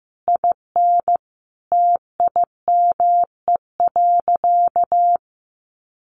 Morse code, heard as a keyed tone, about 15 wpm, 700 Hz. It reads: IN TIME.
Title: Sentences from Top 1400 Words 15wpm